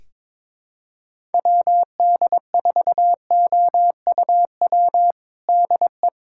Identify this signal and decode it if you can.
Morse code, heard as a keyed tone, about 22 wpm, 700 Hz. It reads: WD4OUW DE
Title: USA Callsign DE Callsign 22wpm